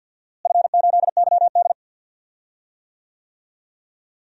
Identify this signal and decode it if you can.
Morse code, heard as a keyed tone, about 53 wpm, 700 Hz. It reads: F8YD
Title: DXCC-100 Call Signs - 4 Characters - Repeated 2x - 50wpm